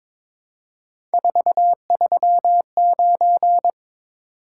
Morse code, heard as a keyed tone, about 22 wpm, 700 Hz. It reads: 439
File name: practice / 22wpm / Three Letter-Number Combinations - 22wpm